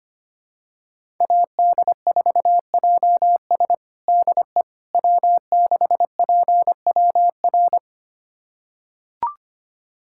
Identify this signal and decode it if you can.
Morse code, heard as a keyed tone, about 25 wpm, 700 Hz. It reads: AD4JS DE W6PWR
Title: USA Callsign DE Callsign - 5 Characters - 25wpm